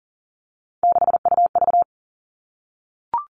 Morse code, heard as a keyed tone, about 40 wpm, 700 Hz. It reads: 6UV
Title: All Cut Numbers - 3-Digit 40wpm